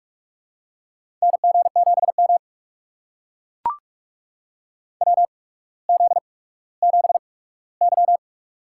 Morse code, heard as a keyed tone, about 44 wpm, spaced as for 19 wpm, 700 Hz. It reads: NG7M WZ7Y
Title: USA Call Signs - 4 Characters - 8x Character Spacing - 45wpm